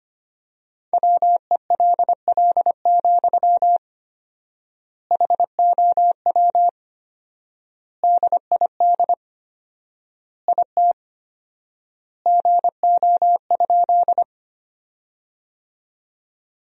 Morse code, heard as a keyed tone, about 25 wpm, 700 Hz. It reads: WELL, HOW DID IT GO?
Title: 4x Word-Spacing - Sentences from Top 200 Words - 25wpm